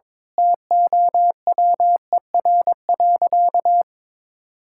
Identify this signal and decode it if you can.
Morse code, heard as a keyed tone, about 21 wpm, 700 Hz. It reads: TOWER.
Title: Pangram Challenge v2 - 22wpm